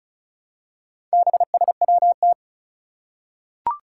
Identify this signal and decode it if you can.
Morse code, heard as a keyed tone, about 36 wpm, 700 Hz. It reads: BSWT
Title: Four Challenging Letter-Number Combinations 35wpm